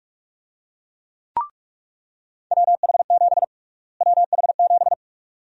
Morse code, heard as a keyed tone, about 43 wpm, 700 Hz. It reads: WH7 WH7
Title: USA Call Sign Prefixes - Repeated 2x - 45wpm